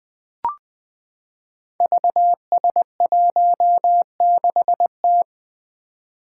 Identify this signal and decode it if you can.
Morse code, heard as a keyed tone, about 21 wpm, 700 Hz. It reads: VS16T
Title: Five Challenging Letter-Number Combinations 20wpm